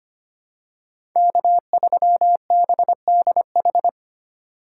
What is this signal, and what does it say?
Morse code, heard as a keyed tone, about 25 wpm, 700 Hz. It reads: K3BDH
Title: USA Call Signs - 5 Characters - 4x Character Spacing - 25wpm